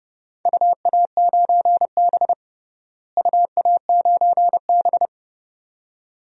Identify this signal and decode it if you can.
Morse code, heard as a keyed tone, about 30 wpm, 700 Hz. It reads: UA9B UA9B
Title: DXCC-100 Call Signs - 4 Characters - Repeated 2x - 30wpm